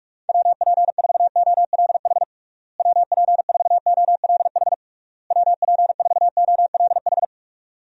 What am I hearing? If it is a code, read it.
Morse code, heard as a keyed tone, about 42 wpm, 700 Hz. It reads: WP4OLH WP4OLH WP4OLH
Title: USA Call Signs - 6 Characters - Repeated 3x - 45wpm